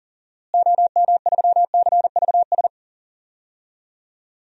Morse code, heard as a keyed tone, about 40 wpm, 700 Hz. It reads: OM3CVS
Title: DXCC-100 Call Signs - 6 Characters - Repeated 3x - 40wpm